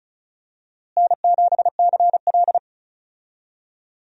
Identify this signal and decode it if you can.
Morse code, heard as a keyed tone, about 36 wpm, 700 Hz. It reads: N7CL
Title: USA Call Signs - 4 Characters - Repeated 3x - 35wpm